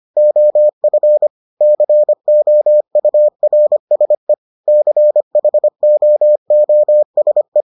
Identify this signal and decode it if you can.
Morse code, heard as a keyed tone, about 25 wpm, 600 Hz. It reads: OF COURSE CHOOSE